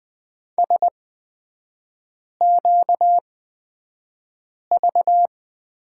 Morse code, heard as a keyed tone, about 20 wpm, spaced as for 5 wpm, 700 Hz. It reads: SQV
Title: 236 Lesson 38e - Letter q - Three Characters 20wpm